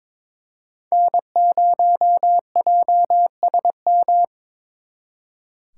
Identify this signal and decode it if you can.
Morse code, heard as a keyed tone, about 26 wpm, 700 Hz. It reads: N0JSM
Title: Speed Racing - USA Call Signs 22wpm